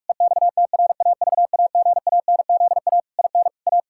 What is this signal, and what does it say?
Morse code, heard as a keyed tone, about 46 wpm, 700 Hz. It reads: EXTRAVAGANZA IN A
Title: Pangram Challenge v2 - 45wpm